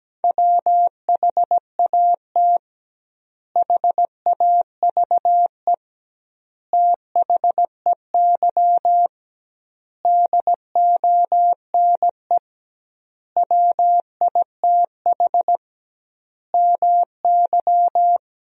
Morse code, heard as a keyed tone, about 17 wpm, 700 Hz. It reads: WHAT HAVE THEY DONE WITH MY
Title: 2x Word-Spacing - Sentences from Top 500 Words - 17wpm